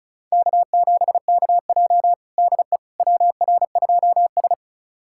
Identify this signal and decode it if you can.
Morse code, heard as a keyed tone, about 36 wpm, 700 Hz. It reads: K7KJ DE WR2S
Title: USA Callsign DE Callsign - 4 Characters - 35wpm